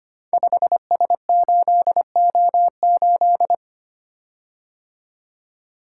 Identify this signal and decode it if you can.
Morse code, heard as a keyed tone, about 25 wpm, 700 Hz. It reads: 5S8O8
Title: Code Groups - 5 Letter:Numbers - 25wpm